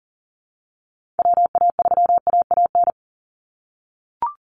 Morse code, heard as a keyed tone, about 40 wpm, 700 Hz. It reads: WA3AAN
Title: Call Signs - 40wpm